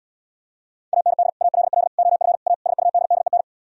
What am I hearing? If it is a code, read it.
Morse code, heard as a keyed tone, about 77 wpm, 690 Hz. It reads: AND ALL OF A SUDDEN